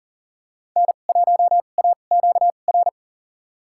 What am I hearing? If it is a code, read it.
Morse code, heard as a keyed tone, about 40 wpm, spaced as for 29 wpm, 700 Hz. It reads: N1AQR
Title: USA Call Signs - 5 Characters - 2x Character Spacing - 40wpm